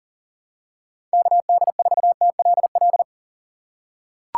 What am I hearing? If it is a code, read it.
Morse code, heard as a keyed tone, about 40 wpm, 700 Hz. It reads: KD4TLL
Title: USA Call Signs - 6 Characters - 3x Character Spacing - 40wpm